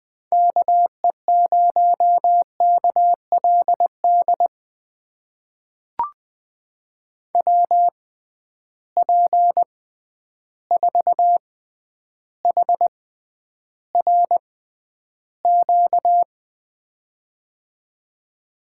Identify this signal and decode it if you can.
Morse code, heard as a keyed tone, about 20 wpm, spaced as for 10 wpm, 700 Hz. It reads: KE0KLD WP4HRQ
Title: USA Call Signs - 6 Characters - 6x Character Spacing - 20wpm